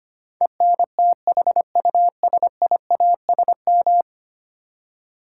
Morse code, heard as a keyed tone, about 25 wpm, 700 Hz. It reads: ENTHUSIASM
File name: 10 Letter Words - First 3 Letters Given - 25wpm